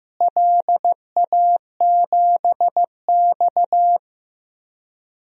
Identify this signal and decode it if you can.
Morse code, heard as a keyed tone, about 15 wpm, 700 Hz. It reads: LA7X
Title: DXCC-100 Call Signs - 4 Characters - Repeated 3x - 15wpm